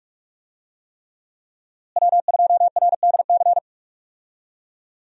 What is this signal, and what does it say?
Morse code, heard as a keyed tone, about 46 wpm, 700 Hz. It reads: W2RDC
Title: USA Call Signs - 5 Characters - Repeated 3x - 45wpm